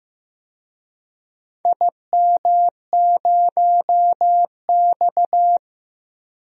Morse code, heard as a keyed tone, about 15 wpm, 700 Hz. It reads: IM0X
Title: DXCC-100 Call Signs - 4 Characters - 15wpm